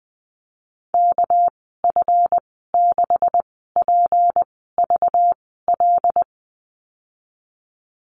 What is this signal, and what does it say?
Morse code, heard as a keyed tone, about 20 wpm, spaced as for 14 wpm, 700 Hz. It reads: KF6PVL
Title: 213 Lesson 34g - Number 8 - Call Signs 20wpm